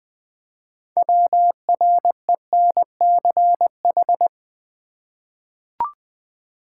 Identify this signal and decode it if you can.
Morse code, heard as a keyed tone, about 20 wpm, 700 Hz. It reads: WRENCH
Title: Easy Words with Hints - 20wpm